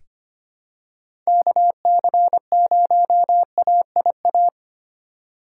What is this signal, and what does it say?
Morse code, heard as a keyed tone, about 25 wpm, 700 Hz. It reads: KC0AIA
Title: USA Call Signs - 2x Character Spacing - 25wpm